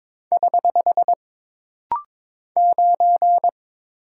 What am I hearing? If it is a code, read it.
Morse code, heard as a keyed tone, about 22 wpm, 700 Hz. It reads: <HH> 9
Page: This rapid-fire timing is geared to push your ICR (Instant Character Recognition) skills.